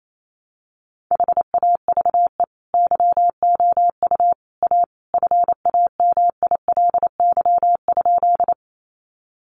Morse code, heard as a keyed tone, about 28 wpm, 700 Hz. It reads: HAVE YOU A FAMILY?
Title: Sentences from Top 400 Words 28wpm